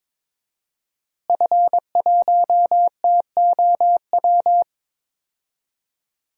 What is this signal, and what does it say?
Morse code, heard as a keyed tone, about 22 wpm, 700 Hz. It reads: F1TOW
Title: DXCC-100 Call Signs - 5 Characters - Repeated 2x - 22wpm